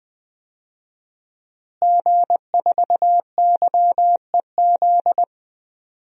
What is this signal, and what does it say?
Morse code, heard as a keyed tone, about 20 wpm, 700 Hz. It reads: G4YEZ
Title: DXCC-100 Call Signs - 5 Characters - Repeated 3x - 20wpm